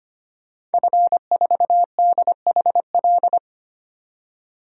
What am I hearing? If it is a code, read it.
Morse code, heard as a keyed tone, about 25 wpm, 700 Hz. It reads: F4DHL
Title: DXCC-100 Call Signs - 5 Characters - 25wpm